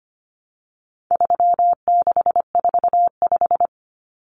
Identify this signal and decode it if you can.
Morse code, heard as a keyed tone, about 25 wpm, 700 Hz. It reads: 3645
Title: Common Cut Numbers - 4-Digit 25wpm